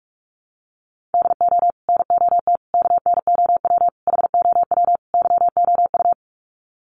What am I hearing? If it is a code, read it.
Morse code, heard as a keyed tone, about 44 wpm, 700 Hz. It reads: DO NOT KNOW HOW YOU
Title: Sets of 5 Words Encore 45wpm